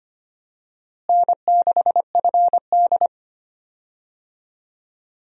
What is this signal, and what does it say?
Morse code, heard as a keyed tone, about 25 wpm, 700 Hz. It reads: N6FD
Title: USA Call Signs - 4 Characters - Repeated 2x - 25wpm